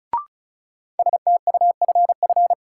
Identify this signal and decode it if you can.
Morse code, heard as a keyed tone, about 36 wpm, 700 Hz. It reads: STUFF
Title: Sentences from Top 1400 Words - New Word Drill - 35wpm